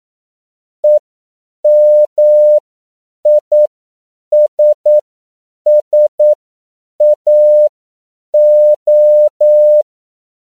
Morse code, heard as a keyed tone, about 9 wpm, spaced as for 7 wpm, 600 Hz. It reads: EMISSAO
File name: CW / 10PPM